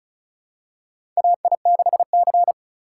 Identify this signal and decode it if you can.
Morse code, heard as a keyed tone, about 35 wpm, 700 Hz. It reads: AI6C
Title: USA Call Signs - 4 Characters - 2x Character Spacing - 35wpm